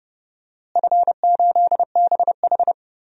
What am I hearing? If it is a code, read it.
Morse code, heard as a keyed tone, about 30 wpm, 700 Hz. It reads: F8BH
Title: DXCC-100 Call Signs - 4 Characters - Repeated 3x - 30wpm